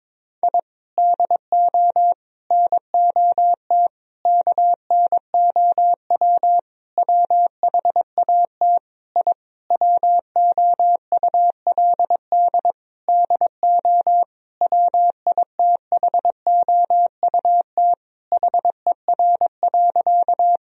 Morse code, heard as a keyed tone, about 22 wpm, 700 Hz. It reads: I DO NOT KNOW WHAT I WOULD DO WITHOUT HER.
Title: Sentences from Top 1000 Words - Review Part 3 - 22wpm